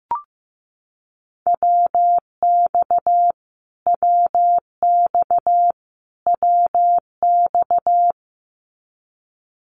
Morse code, heard as a keyed tone, about 15 wpm, 700 Hz. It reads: WX WX WX
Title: Abbreviations Repeated 3x - 15wpm